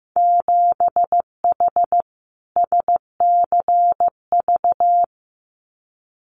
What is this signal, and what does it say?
Morse code, heard as a keyed tone, about 15 wpm, 700 Hz. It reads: 7H SCV
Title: Winter Field Day Exchanges Encore 15wpm